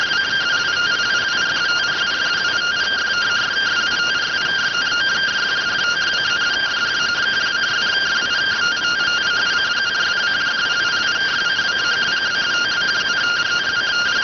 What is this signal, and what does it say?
Two-tone teleprinter signal, about 170 Hz shift.